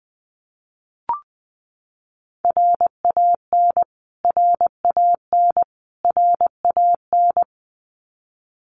Morse code, heard as a keyed tone, about 19 wpm, 700 Hz. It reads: RAN RAN RAN
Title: Top 500 Words Repeated 20wpm